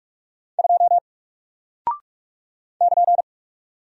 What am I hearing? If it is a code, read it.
Morse code, heard as a keyed tone, about 44 wpm, 700 Hz. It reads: 2 <KN>
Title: Single Character (Letter:Number:Punctuation:Prosign) Alt 45wpm